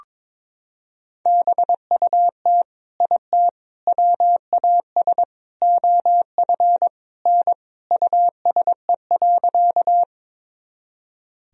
Morse code, heard as a keyed tone, about 22 wpm, 700 Hz. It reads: BUT IT WAS OF N USE.
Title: Sentences from Top 100 Words and Abbreviations 22wpm